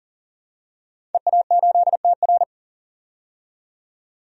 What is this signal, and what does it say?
Morse code, heard as a keyed tone, about 40 wpm, 700 Hz. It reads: EA8TR
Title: DXCC-100 Call Signs - 5 Characters - 40wpm